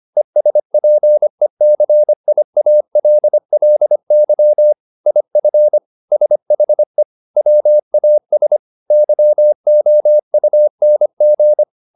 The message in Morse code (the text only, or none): ESPECIALLY IF SHE WAS YOUNG